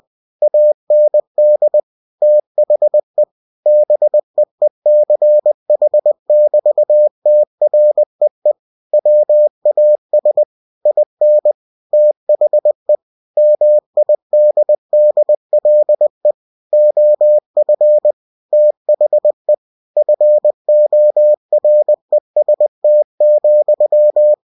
Morse code, heard as a keyed tone, about 20 wpm, 600 Hz. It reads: AND THE BEECH<BT>TREE WAS IN THE MIDDLE OF THE FOREST,
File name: cw-books / winnie-the-pooh / winnie-20wpm-page-0020